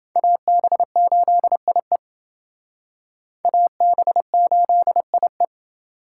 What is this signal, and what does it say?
Morse code, heard as a keyed tone, about 28 wpm, 700 Hz. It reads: AB8IE AB8IE